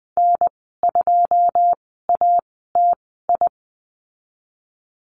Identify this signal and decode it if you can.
Morse code, heard as a keyed tone, about 20 wpm, spaced as for 15 wpm, 700 Hz. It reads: N2ATI
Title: 161 Lesson 26g - Letter g - Call Signs 20wpm